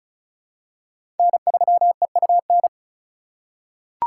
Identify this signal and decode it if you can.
Morse code, heard as a keyed tone, about 36 wpm, 700 Hz. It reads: N3EUN
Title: USA Call Signs - 4x Character Spacing - 35wpm